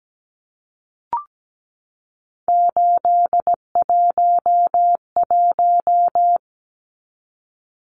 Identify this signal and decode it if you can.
Morse code, heard as a keyed tone, about 17 wpm, 700 Hz. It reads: 811
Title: Numbers - 3-Digit 17wpm